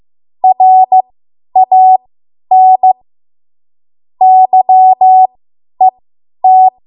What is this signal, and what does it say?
Morse code, heard as a keyed tone, about 15 wpm, spaced as for 10 wpm, 750 Hz. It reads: RAN YET
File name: Morse / 10WPM